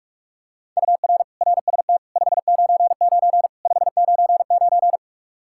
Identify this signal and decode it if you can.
Morse code, heard as a keyed tone, about 43 wpm, 700 Hz. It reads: UR RST 599 599
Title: K1USN SST QSO Practice 45wpm